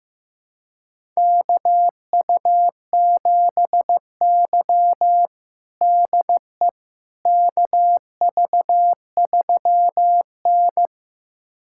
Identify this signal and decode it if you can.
Morse code, heard as a keyed tone, about 15 wpm, 700 Hz. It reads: KU7Y DE KV3N
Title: USA Callsign DE Callsign - 4 Characters - 15wpm